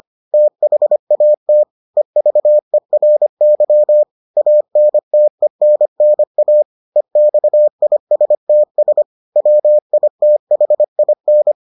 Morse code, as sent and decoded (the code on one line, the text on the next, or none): - .... .- - / . ...- . .-. -.-- / .- -. - . -. -. .- / . -..- .. ... - ... / .-- .. - .... .. -.
THAT EVERY ANTENNA EXISTS WITHIN